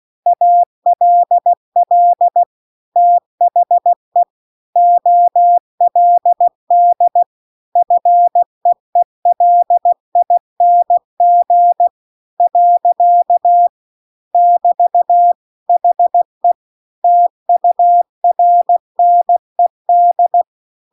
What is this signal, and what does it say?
Morse code, heard as a keyed tone, about 16 wpm, 700 Hz. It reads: ALL THE OLD FEELING . <BT> HE TURNED